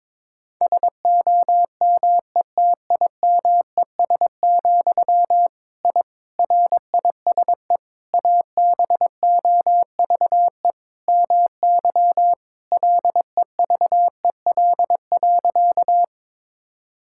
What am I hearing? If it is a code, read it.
Morse code, heard as a keyed tone, about 22 wpm, 700 Hz. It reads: SOMETIMES, I RISE ABOVE MY LEVEL.
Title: Sentences from Top 1000 Words - Review Part 6 - 22wpm